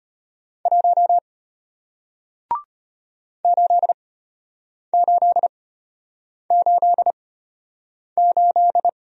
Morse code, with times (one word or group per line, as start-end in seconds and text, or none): 0.7-1.2 1
3.4-8.9 8888